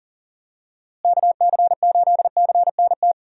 Morse code, heard as a keyed tone, about 40 wpm, 700 Hz. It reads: KC8CNT
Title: USA Call Signs - 6 Characters - Repeated 2x - 40wpm